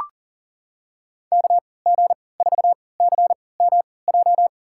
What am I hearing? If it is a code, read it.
Morse code, heard as a keyed tone, about 40 wpm, spaced as for 23 wpm, 700 Hz. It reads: KG4CMJ
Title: USA Call Signs - 6 Characters - 3x Character Spacing - 40wpm